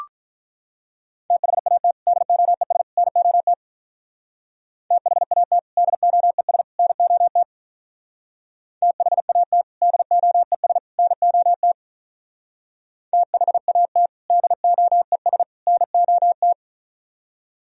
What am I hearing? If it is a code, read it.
Morse code, heard as a keyed tone, about 42 wpm, 700 Hz. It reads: THAT DOES NOT THAT DOES NOT THAT DOES NOT THAT DOES NOT
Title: Speed-Racing - Sets of 3 Words 35wpm